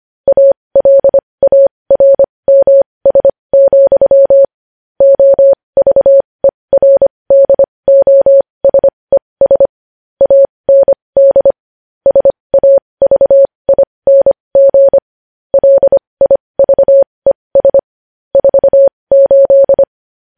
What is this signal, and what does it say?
Morse code, heard as a keyed tone, about 25 wpm, spaced as for 20 wpm, 550 Hz. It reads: ALARMS, OVERDOSES AND SAVING LIVES 48
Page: news-headlines-in-morse-code-at-20-wpm